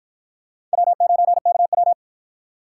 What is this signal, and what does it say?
Morse code, heard as a keyed tone, about 53 wpm, 700 Hz. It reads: W9KW